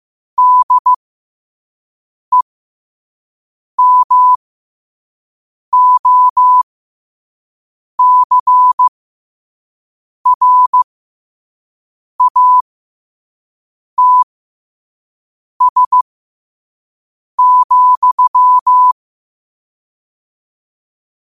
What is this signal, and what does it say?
Morse code, heard as a keyed tone, about 15 wpm, spaced as for 5 wpm, 1000 Hz. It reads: DEMOCRATS,
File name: qotdfiles / 5wpm